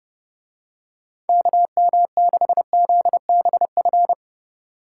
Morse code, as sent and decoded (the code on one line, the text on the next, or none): -.- -- -.... --.. -... ..-.
KM6ZBF